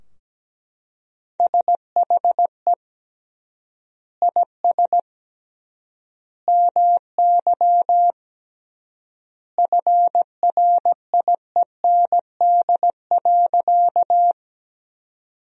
Morse code, heard as a keyed tone, about 17 wpm, 700 Hz. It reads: SHE IS MY FRIEND.
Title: 3x Word-Spacing - Sentences from Top 300 Words - 17wpm